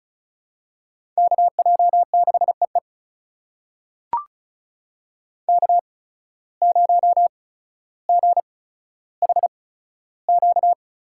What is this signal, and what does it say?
Morse code, heard as a keyed tone, about 36 wpm, spaced as for 15 wpm, 700 Hz. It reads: KJ6EE K0GHQ
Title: USA Call Signs - 5 Characters - 8x Character Spacing - 35wpm